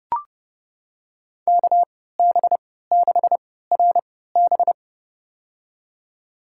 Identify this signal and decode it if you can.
Morse code, heard as a keyed tone, about 30 wpm, spaced as for 17 wpm, 700 Hz. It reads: KB6RB
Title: USA Call Signs - 5 Characters - 3x Character Spacing - 30wpm